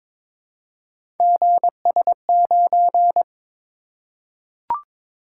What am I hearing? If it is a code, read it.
Morse code, heard as a keyed tone, about 22 wpm, 700 Hz. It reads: GS9
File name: practice / 22wpm / Three Letter-Number Combinations - 22wpm